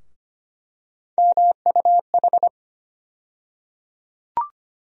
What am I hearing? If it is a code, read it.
Morse code, heard as a keyed tone, about 25 wpm, 700 Hz. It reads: MUH
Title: USA Call Sign Postfixes - 3 Letters 25wpm